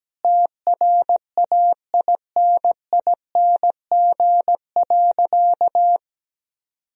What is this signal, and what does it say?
Morse code, heard as a keyed tone, about 17 wpm, 700 Hz. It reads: TRAINING.
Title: Sentences from Top 1000 Words - Review Part 4 - 17wpm